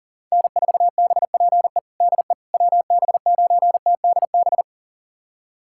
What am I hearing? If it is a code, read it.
Morse code, heard as a keyed tone, about 40 wpm, 700 Hz. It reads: N4BPE DE WB9TDB
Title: USA Callsign DE Callsign 40wpm